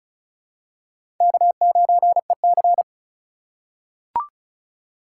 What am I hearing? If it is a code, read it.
Morse code, heard as a keyed tone, about 34 wpm, 700 Hz. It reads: K9EC
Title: USA Call Signs - 2x Character Spacing - 35wpm